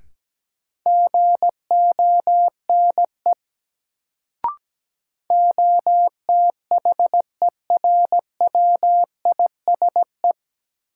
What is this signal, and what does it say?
Morse code, heard as a keyed tone, about 17 wpm, 700 Hz. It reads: GONE OTHERWISE
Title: Sentences from Top 1200 Words - New Word Drill Alt - 17wpm